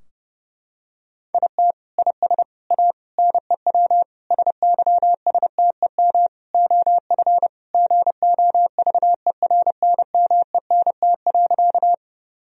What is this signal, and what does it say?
Morse code, heard as a keyed tone, about 32 wpm, 700 Hz. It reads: IT IS A NEW SYSTEM OF GOVERNMENT.
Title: Sentences from Top 1000 Words - Review Part 3 - 30wpm